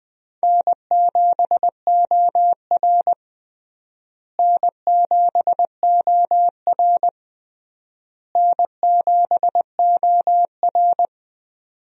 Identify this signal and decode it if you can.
Morse code, heard as a keyed tone, about 20 wpm, 700 Hz. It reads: N7OR N7OR N7OR
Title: USA Call Signs - 4 Characters - Repeated 3x - 20wpm